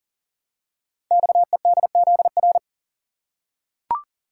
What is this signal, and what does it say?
Morse code, heard as a keyed tone, about 42 wpm, 700 Hz. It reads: XEDZR
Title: Code Groups - 5 Letters - 40wpm